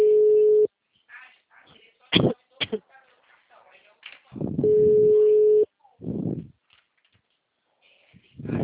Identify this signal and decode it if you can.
Ringback tone (Europe, 425 Hz), with speech.